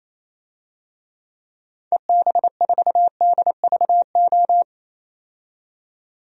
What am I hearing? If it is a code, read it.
Morse code, heard as a keyed tone, about 27 wpm, 700 Hz. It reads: EB4DVO
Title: DXCC-100 Call Signs - 6 Characters - 28wpm